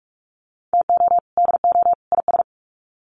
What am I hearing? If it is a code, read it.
Morse code, heard as a keyed tone, about 44 wpm, 700 Hz. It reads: TO DO IS
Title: Sets of 3 Words 45wpm